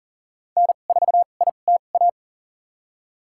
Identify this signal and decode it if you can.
Morse code, heard as a keyed tone, about 40 wpm, spaced as for 29 wpm, 700 Hz. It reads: N4ITA
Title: USA Call Signs - 5 Characters - 2x Character Spacing - 40wpm